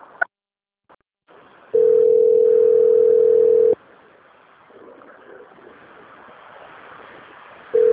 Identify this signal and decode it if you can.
Ringback tone (North America), 440 + 480 Hz.
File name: rg-1600-unknown- - - .3388